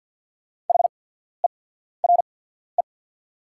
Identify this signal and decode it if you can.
Morse code, heard as a keyed tone, about 48 wpm, spaced as for 14 wpm, 690 Hz. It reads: HERE
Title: 8x Character Spacing - Top 200 Words - 50wpm